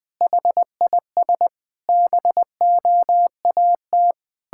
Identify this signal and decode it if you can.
Morse code, heard as a keyed tone, about 21 wpm, 700 Hz. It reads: HIS BOAT
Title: Pangram Challenge v2 - 20wpm